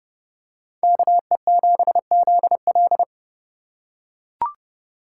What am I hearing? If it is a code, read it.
Morse code, heard as a keyed tone, about 29 wpm, 700 Hz. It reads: KE7ZL